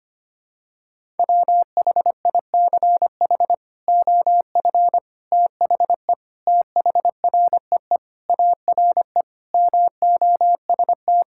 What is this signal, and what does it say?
Morse code, heard as a keyed tone, about 25 wpm, 700 Hz. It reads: WHICH OF THE THREE ARE MOST
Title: Sentences from Top 1000 Words - Review Part 5 - 25wpm